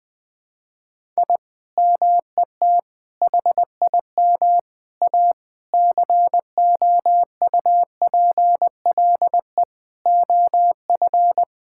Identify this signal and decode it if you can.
Morse code, heard as a keyed tone, about 20 wpm, 700 Hz. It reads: I MET HIM A COUPLE OF
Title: Sentences with 5-Word Phrases - 20wpm